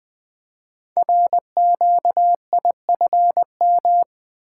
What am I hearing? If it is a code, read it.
Morse code, heard as a keyed tone, about 20 wpm, 700 Hz. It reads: RQIFM
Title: Code Groups - 5 Letters - 20wpm